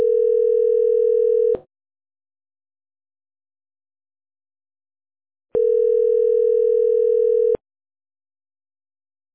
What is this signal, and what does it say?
Ringback tone (North America), 440 + 480 Hz.